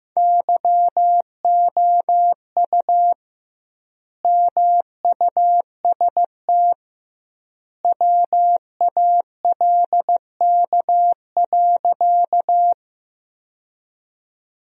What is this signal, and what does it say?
Morse code, heard as a keyed tone, about 16 wpm, 700 Hz. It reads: YOU MUST WALK.
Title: 2x Word-Spacing - Sentences from Top 300 Words - 15wpm